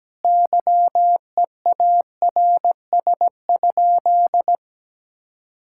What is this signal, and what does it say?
Morse code, heard as a keyed tone, about 17 wpm, 700 Hz. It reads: YEARS?
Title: Sentences from Top 1000 Words - Review Part 6 - 17wpm